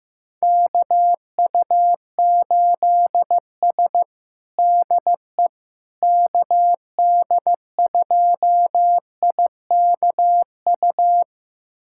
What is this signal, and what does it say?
Morse code, heard as a keyed tone, about 15 wpm, 700 Hz. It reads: KU8S DE KD2IKU
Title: USA Callsign DE Callsign 15wpm